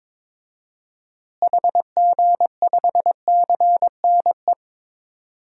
Morse code, heard as a keyed tone, about 22 wpm, 700 Hz. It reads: HG5CNE
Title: DXCC-100 Call Signs - 6 Characters - Repeated 2x - 22wpm